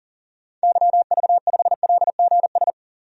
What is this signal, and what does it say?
Morse code, heard as a keyed tone, about 40 wpm, 700 Hz. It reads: YV5LGS
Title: DXCC-100 Call Signs - 6 Characters - 40wpm